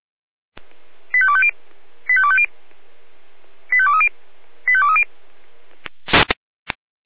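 Five-tone selective call (ZVEI), heard as quick runs of five tones, tones of about 70 ms. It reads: 85250 85250 85230 85230
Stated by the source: ZVEI five-tone selective call 85250 received via SDR by the Orlamünde fire brigade (feuerwehr-orlamuende).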